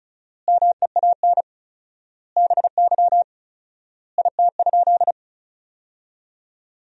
Morse code, heard as a keyed tone, about 36 wpm, 700 Hz. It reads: MEAN BY IT?
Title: 4x Word-Spacing - Sentences from Top 200 Words - 35wpm